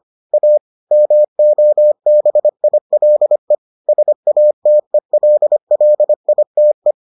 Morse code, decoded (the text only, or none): A MOBILE SATELLITE